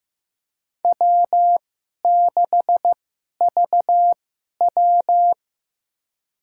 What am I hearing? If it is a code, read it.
Morse code, heard as a keyed tone, about 15 wpm, spaced as for 11 wpm, 700 Hz. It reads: W6VW
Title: USA Call Signs - 4 Characters - 2x Character Spacing - 15wpm